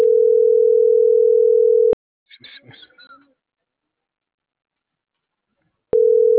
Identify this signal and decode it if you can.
Ringback tone (North America), 440 + 480 Hz.